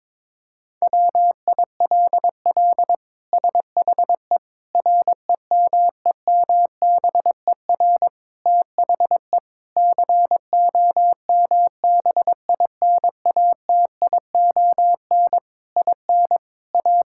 Morse code, heard as a keyed tone, about 22 wpm, 700 Hz. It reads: WILL SHE REMEMBER THE COMBINATION IN A
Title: Sentences from Top 1000 Words - Review Part 6 - 22wpm